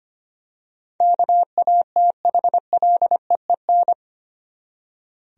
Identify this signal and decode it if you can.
Morse code, heard as a keyed tone, about 25 wpm, 700 Hz. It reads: KATHLEEN
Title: USA Common Names - 25wpm